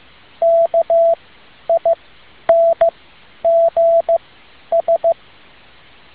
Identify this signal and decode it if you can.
Morse code, heard as a keyed tone, about 15 wpm, spaced as for 10 wpm, 670 Hz. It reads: KINGS
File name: tin-upgrade / QSO 2013 02 12